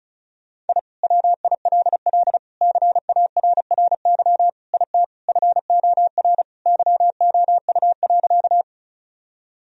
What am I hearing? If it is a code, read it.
Morse code, heard as a keyed tone, about 34 wpm, 700 Hz. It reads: I WILL CARRY IT FOR YOU.
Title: Sentences from Top 1000 Words - Review Part 1 - 35wpm